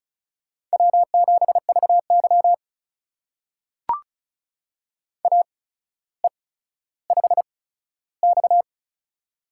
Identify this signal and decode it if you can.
Morse code, heard as a keyed tone, about 36 wpm, spaced as for 15 wpm, 700 Hz. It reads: W7VY AE5X
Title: USA Call Signs - 4 Characters - 8x Character Spacing - 35wpm